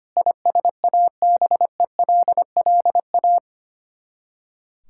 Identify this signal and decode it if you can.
Morse code, heard as a keyed tone, about 25 wpm, 700 Hz. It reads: ISABELLA
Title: K1USN SST QSO Practice 25wpm